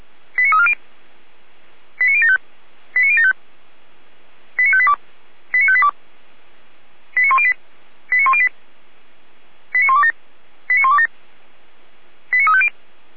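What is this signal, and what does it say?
Five-tone selective call (ZVEI), heard as quick runs of five tones, tones of about 70 ms. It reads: 89340 89075 89075 89582 89582 89108 89108 89127 89127 89350
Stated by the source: ZVEI five-tone selective call received with an SDR.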